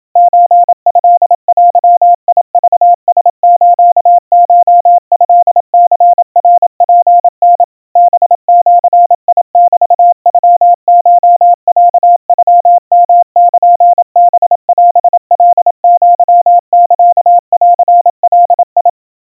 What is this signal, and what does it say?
Morse code (Japanese Wabun code), heard as a keyed tone, about 27 wpm, 700 Hz. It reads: ソトデクラスコトニナツタ ハジメノコロノヨルハオカアサンガ
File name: kawa / Morse sample / kamyu 100 cpm 700 Hz Crispy